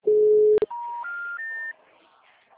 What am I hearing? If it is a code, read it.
Special information tone.